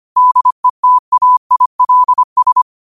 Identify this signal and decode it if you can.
Morse code, heard as a keyed tone, about 25 wpm, 1000 Hz. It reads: DETAILS
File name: qotdfiles / 25wpm